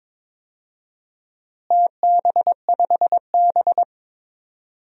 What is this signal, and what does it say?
Morse code, heard as a keyed tone, about 22 wpm, 700 Hz. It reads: TB5B
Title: DXCC-100 Call Signs - 4 Characters - Repeated 2x - 22wpm